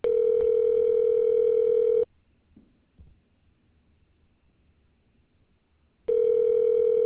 Ringback tone (North America), 440 + 480 Hz.